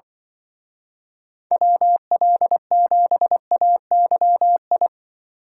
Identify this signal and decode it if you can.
Morse code, heard as a keyed tone, about 23 wpm, 700 Hz. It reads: WL7AYI
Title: Speed Racing - USA Call Signs 20wpm